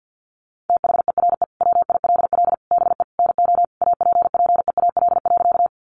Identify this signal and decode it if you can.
Morse code, heard as a keyed tone, about 50 wpm, 700 Hz. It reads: THERE WILL BE NO APPEAL.
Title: Sentences from Top 1300 Words 50wpm